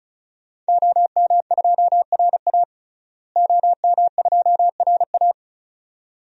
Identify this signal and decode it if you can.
Morse code, heard as a keyed tone, about 36 wpm, 700 Hz. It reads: OM2RA OM2RA
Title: DXCC-100 Call Signs - 5 Characters - Repeated 2x - 35wpm